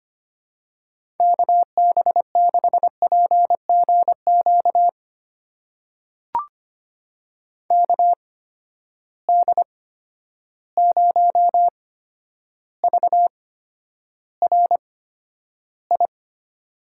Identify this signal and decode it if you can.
Morse code, heard as a keyed tone, about 25 wpm, spaced as for 11 wpm, 700 Hz. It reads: KB6PGQ KD0VRI
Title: USA Call Signs - 6 Characters - 8x Character Spacing - 25wpm